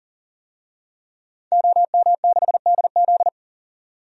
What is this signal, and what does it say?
Morse code, heard as a keyed tone, about 40 wpm, 700 Hz. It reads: OM6DZ
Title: DXCC-100 Call Signs - 5 Characters - 40wpm